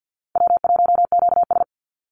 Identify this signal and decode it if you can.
Morse code, heard as a keyed tone, about 48 wpm, 700 Hz. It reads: W1QS
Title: Call Signs Repeated 3x - 50wpm